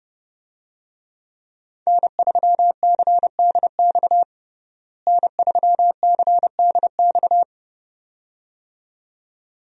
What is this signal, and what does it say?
Morse code, heard as a keyed tone, about 30 wpm, 700 Hz. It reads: N3CDX N3CDX
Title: USA Call Signs - 5 Characters - Repeated 2x - 30wpm